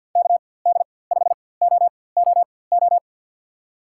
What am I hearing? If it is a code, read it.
Morse code, heard as a keyed tone, about 51 wpm, spaced as for 23 wpm, 700 Hz. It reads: KD5OOO
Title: USA Call Signs - 6 Characters - 4x Character Spacing - 50wpm